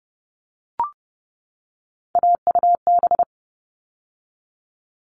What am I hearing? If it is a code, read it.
Morse code, heard as a keyed tone, about 30 wpm, 700 Hz. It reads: AUB
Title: All Cut Numbers - 3-Digit 30wpm